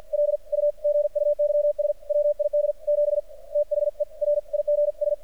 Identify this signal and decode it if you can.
Morse code, heard as a keyed tone, about 49 wpm, 590 Hz. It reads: L U GUYS WID 7 THE RIGS
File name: archived / CRSnet